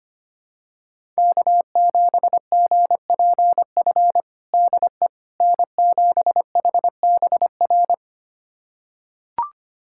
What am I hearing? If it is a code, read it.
Morse code, heard as a keyed tone, about 25 wpm, 700 Hz. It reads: K7GPF DE N7HBR
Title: USA Callsign DE Callsign - 5 Characters - 25wpm